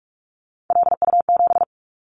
Morse code, heard as a keyed tone, about 46 wpm, 700 Hz. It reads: LU7
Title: Callsign Prefixes for Top 50 DXCC - 45wpm